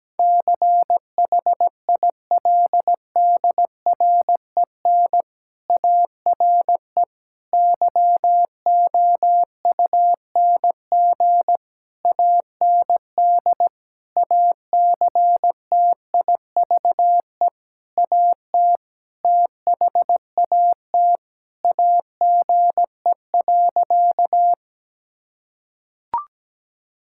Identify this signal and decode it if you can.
Morse code, heard as a keyed tone, about 17 wpm, 700 Hz. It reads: CHILDREN ARE YOUNG AND ACTIVE AT THAT AGE.
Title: Sentences from Top 1000 Words - Review Part 6 - 17wpm